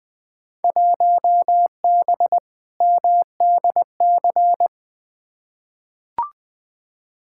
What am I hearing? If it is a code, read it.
Morse code, heard as a keyed tone, about 20 wpm, 700 Hz. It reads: 1B MDC
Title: Field Day Exchanges Encore - 20wpm